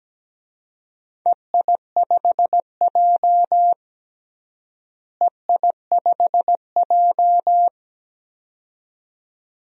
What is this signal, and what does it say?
Morse code, heard as a keyed tone, about 17 wpm, 700 Hz. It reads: EI5J EI5J
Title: DXCC-100 Call Signs - 4 Characters - Repeated 2x - 17wpm